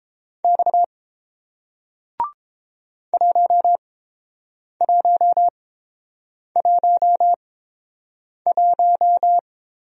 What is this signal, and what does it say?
Morse code, heard as a keyed tone, about 30 wpm, spaced as for 7 wpm, 700 Hz. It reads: X 1111